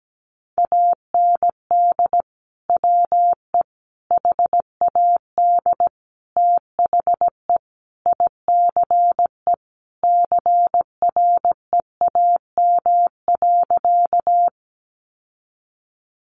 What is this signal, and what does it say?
Morse code, heard as a keyed tone, about 17 wpm, 700 Hz. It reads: AND WE HAD THE ICE CREAM.
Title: Sentences from Top 1500 Words 17wpm